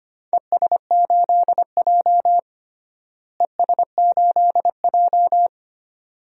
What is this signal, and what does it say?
Morse code, heard as a keyed tone, about 25 wpm, 700 Hz. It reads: ES8J ES8J
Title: DXCC-100 Call Signs - 4 Characters - Repeated 2x - 25wpm